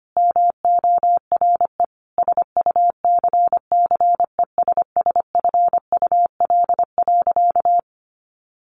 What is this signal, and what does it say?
Morse code, heard as a keyed tone, about 25 wpm, 700 Hz. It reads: MORE SUCCESSFUL.
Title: Sentences from Top 900 Words 25wpm